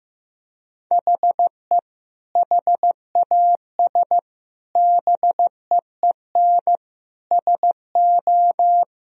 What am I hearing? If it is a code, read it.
Morse code, heard as a keyed tone, about 15 wpm, 700 Hz. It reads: HE HAS BEEN SO
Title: Sentences from Top 1000 Words - Review Part 1 - 15wpm